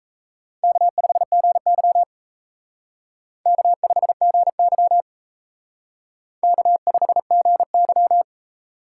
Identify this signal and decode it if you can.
Morse code, heard as a keyed tone, about 34 wpm, 700 Hz. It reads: K5GY K5GY K5GY
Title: Speed Racing - USA 1x2 Call Signs 28wpm